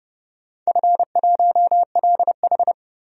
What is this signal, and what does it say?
Morse code, heard as a keyed tone, about 30 wpm, 700 Hz. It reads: F1LH
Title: DXCC-100 Call Signs - 4 Characters - 30wpm